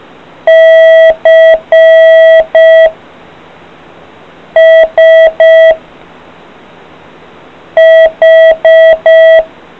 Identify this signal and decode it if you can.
Morse code, heard as a keyed tone, about 5 wpm, 650 Hz. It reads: C S H